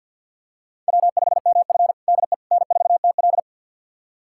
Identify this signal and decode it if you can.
Morse code, heard as a keyed tone, about 50 wpm, 700 Hz. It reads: W5MF DE N4TL
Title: USA Callsign DE Callsign - 4 Characters - 50wpm